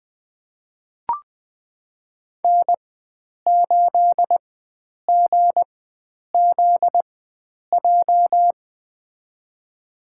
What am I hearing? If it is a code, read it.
Morse code, heard as a keyed tone, about 20 wpm, spaced as for 9 wpm, 700 Hz. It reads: N8GZJ
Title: USA Call Signs - 5 Characters - 4x Character Spacing - 20wpm